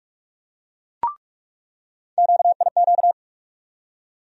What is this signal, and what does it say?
Morse code, heard as a keyed tone, about 46 wpm, 700 Hz. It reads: QIQ
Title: USA Call Sign Postfixes - 3 Letters 45wpm